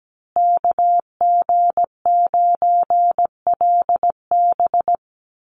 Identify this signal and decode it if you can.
Morse code, heard as a keyed tone, about 17 wpm, 700 Hz. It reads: KG9LB
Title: Call Signs Repeated 3x - 17wpm